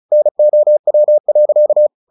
Morse code, heard as a keyed tone, about 34 wpm, 600 Hz. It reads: NOW.
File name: the old man and the sea / the old man and the sea 1 35wpm